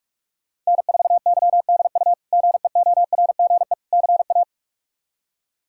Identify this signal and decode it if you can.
Morse code, heard as a keyed tone, about 42 wpm, 700 Hz. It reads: N4YDU GEORGE CA
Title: K1USN SST QSO Practice - 45wpm